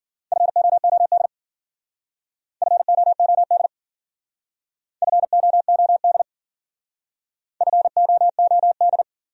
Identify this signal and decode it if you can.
Morse code, heard as a keyed tone, about 50 wpm, 700 Hz. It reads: FOOD FOOD FOOD FOOD
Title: Speed-Racing Top 500 Words 40wpm